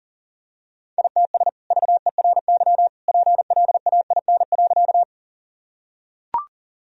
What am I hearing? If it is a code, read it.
Morse code, heard as a keyed tone, about 40 wpm, 700 Hz. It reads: ITS VERY PLAIN.
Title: Sentences from Top 1000 Words - Review Part 2 - 40wpm